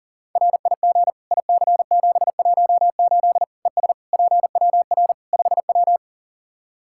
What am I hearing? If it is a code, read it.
Morse code, heard as a keyed tone, about 37 wpm, 700 Hz. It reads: RIG IC718 ES PWR 5W
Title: Practice QSO - Exchange 2 - Info - 40wpm